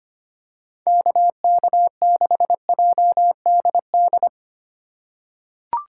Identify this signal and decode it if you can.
Morse code, heard as a keyed tone, about 25 wpm, 700 Hz. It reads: KK6JDD
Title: USA Call Signs - 3x Character Spacing - 25wpm